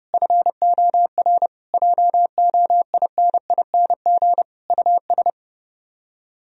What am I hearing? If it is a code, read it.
Morse code, heard as a keyed tone, about 31 wpm, 700 Hz. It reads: FOR JOINING US